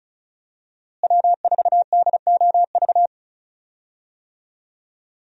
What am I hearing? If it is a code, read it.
Morse code, heard as a keyed tone, about 36 wpm, 700 Hz. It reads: W4DOV
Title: USA Call Signs - 5 Characters - Repeated 3x - 35wpm